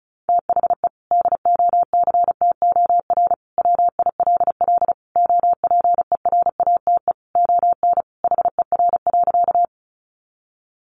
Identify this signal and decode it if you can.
Morse code, heard as a keyed tone, about 34 wpm, 700 Hz. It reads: THE DOCTOR WILL OPERATE ON HER.
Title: Sentences from Top 1500 Words 35wpm